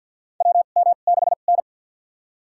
Morse code, heard as a keyed tone, about 46 wpm, spaced as for 34 wpm, 700 Hz. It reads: WM6N
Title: USA Call Signs - 4 Characters - 2x Character Spacing - 50wpm